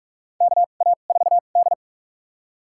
Morse code, heard as a keyed tone, about 44 wpm, spaced as for 32 wpm, 700 Hz. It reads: KA4D
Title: USA Call Signs - 4 Characters - 2x Character Spacing - 45wpm